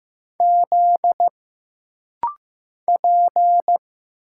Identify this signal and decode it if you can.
Morse code, heard as a keyed tone, about 15 wpm, 700 Hz. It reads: Z P
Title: Single Letters - Rapid-Fire 15wpm